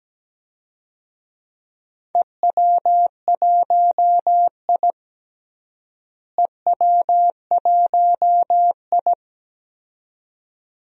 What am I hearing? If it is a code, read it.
Morse code, heard as a keyed tone, about 17 wpm, 700 Hz. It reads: EW1I EW1I
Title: DXCC-100 Call Signs - 4 Characters - Repeated 2x - 17wpm